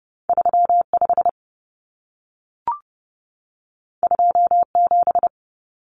Morse code, heard as a keyed tone, about 30 wpm, 700 Hz. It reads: 35 27
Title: Common Cut Numbers - 2-Digit 30wpm